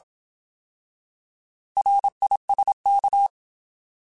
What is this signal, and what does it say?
Morse code, heard as a keyed tone, about 27 wpm, 800 Hz. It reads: RISK
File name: Alphabetical / Alphabetical mp3s / Rs @25